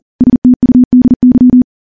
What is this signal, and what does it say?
Morse code, heard as a keyed tone, about 40 wpm, 250 Hz. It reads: STUDY
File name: morse / morse-training-40WPM